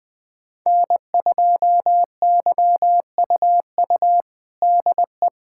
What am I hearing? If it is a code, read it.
Morse code, heard as a keyed tone, about 20 wpm, 700 Hz. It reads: N2YUU DE
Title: USA Callsign DE Callsign - 5 Characters - 20wpm